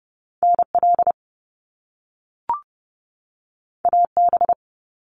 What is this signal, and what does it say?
Morse code, heard as a keyed tone, about 30 wpm, 700 Hz. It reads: NL AB
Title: Canadian Provinces and Territories 30wpm